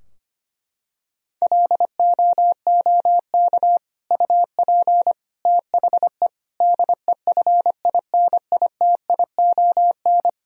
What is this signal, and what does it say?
Morse code, heard as a keyed tone, about 25 wpm, 700 Hz. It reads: LOOK UP THE DEFINITION
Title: Sentences from Top 1000 Words - Review Part 4 - 25wpm